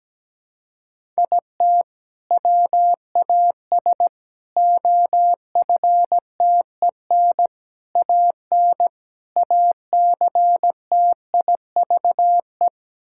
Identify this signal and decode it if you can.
Morse code, heard as a keyed tone, about 17 wpm, 700 Hz. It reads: IT WAS OFTEN AN ACTIVE
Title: Sentences from Top 1000 Words - Review Part 6 - 17wpm